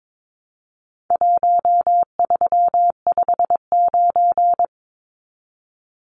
Morse code, heard as a keyed tone, about 22 wpm, 700 Hz. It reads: 1359
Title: Numbers - Contest Style - 4-Digit 22wpm